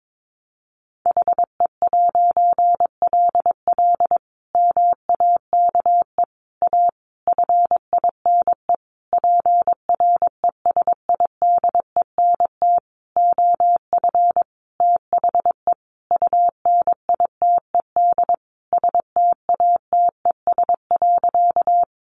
Morse code, heard as a keyed tone, about 22 wpm, 700 Hz. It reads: HE'LL MAKE A FINE PRESIDENT OF THE UNITED STATES.
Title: A Sound Of Thunder 22wpm